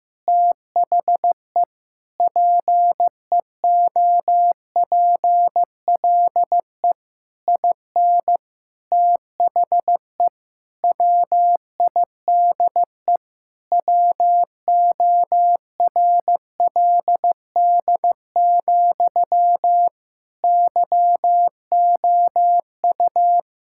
Morse code, heard as a keyed tone, about 15 wpm, 700 Hz. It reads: THE PEOPLE IN THE WIDE WORLD, YOU
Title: Sentences from Top 1000 Words - Review Part 6 - 15wpm